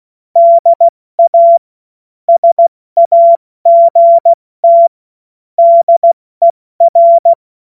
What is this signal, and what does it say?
Morse code, heard as a keyed tone, about 16 wpm, spaced as for 14 wpm, 680 Hz. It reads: DA SAGT DER
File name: Morsekurs / Witze / Witz 16. Februar 2021, 70 F80